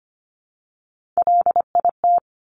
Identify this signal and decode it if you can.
Morse code, heard as a keyed tone, about 25 wpm, 700 Hz. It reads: LIT
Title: Top Three Letter Combinations in Words 25wpm